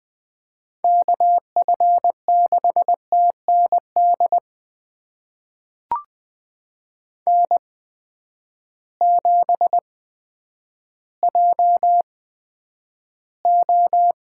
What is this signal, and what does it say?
Morse code, heard as a keyed tone, about 20 wpm, spaced as for 10 wpm, 700 Hz. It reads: KF6TND N7JO
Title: USA Call Signs - 8x Character Spacing - 20wpm